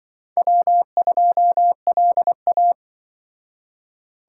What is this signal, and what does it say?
Morse code, heard as a keyed tone, about 25 wpm, 700 Hz. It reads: W2LA